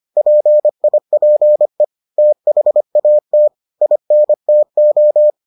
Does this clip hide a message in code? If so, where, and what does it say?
Yes: Morse code from 0.2 to 5.3 s, heard as a keyed tone, about 25 wpm, 600 Hz. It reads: PIPE THAT INTO